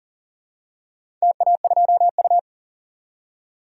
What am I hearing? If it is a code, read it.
Morse code, heard as a keyed tone, about 40 wpm, 700 Hz. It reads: TA2U
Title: DXCC-100 Call Signs - 4 Characters - 40wpm